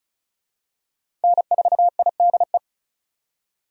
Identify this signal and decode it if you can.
Morse code, heard as a keyed tone, about 36 wpm, 700 Hz. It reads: N4IDE
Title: USA Call Signs - 5 Characters - 35wpm